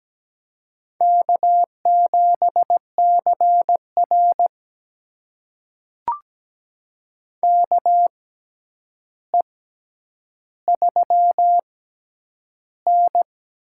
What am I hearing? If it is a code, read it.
Morse code, heard as a keyed tone, about 17 wpm, spaced as for 9 wpm, 700 Hz. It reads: K7CR KE3N
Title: USA Call Signs - 4 Characters - 6x Character Spacing - 17wpm